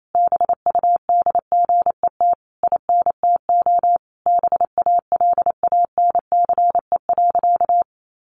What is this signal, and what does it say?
Morse code, heard as a keyed tone, about 28 wpm, 700 Hz. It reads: BUDGET INTO BALANCE.
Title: Sentences from Top 1200 Words 28wpm